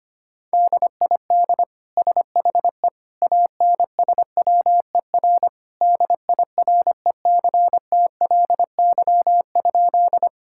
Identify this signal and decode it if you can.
Morse code, heard as a keyed tone, about 25 wpm, 700 Hz. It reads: DID SHE ANSWER DIRECTLY?
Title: Sentences from Top 1000 Words - Review Part 5 - 25wpm